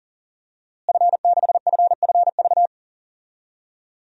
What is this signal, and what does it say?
Morse code, heard as a keyed tone, about 40 wpm, 700 Hz. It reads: F6FFV
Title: DXCC-100 Call Signs - 5 Characters - Repeated 2x - 40wpm